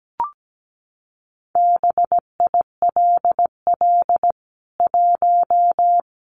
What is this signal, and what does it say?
Morse code, heard as a keyed tone, about 17 wpm, 700 Hz. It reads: BILL 1
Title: CWT Exchange 17wpm